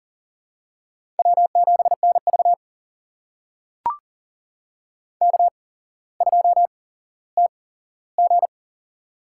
Morse code, heard as a keyed tone, about 40 wpm, spaced as for 17 wpm, 700 Hz. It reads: W7NV K2TG
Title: USA Call Signs - 4 Characters - 8x Character Spacing - 40wpm